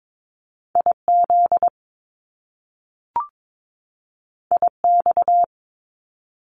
Morse code, heard as a keyed tone, about 22 wpm, 700 Hz. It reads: IZ IX
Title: Top Two Letter Combinations in Words 22wpm